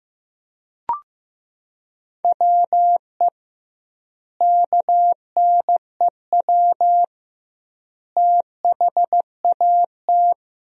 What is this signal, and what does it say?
Morse code, heard as a keyed tone, about 15 wpm, 700 Hz. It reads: WE KNEW THAT
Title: 2x Word-Spacing - Sentences from Top 400 Words - 15wpm